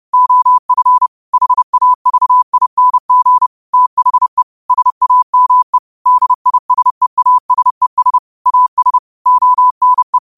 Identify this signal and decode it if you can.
Morse code, heard as a keyed tone, about 30 wpm, 1000 Hz. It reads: OF HAVING THE SAME DISEASES AS ONE
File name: 30wpm